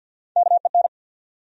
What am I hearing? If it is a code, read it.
Morse code, heard as a keyed tone, about 53 wpm, 700 Hz. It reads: KEN
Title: USA Common Names - Max 3 Letters - 50wpm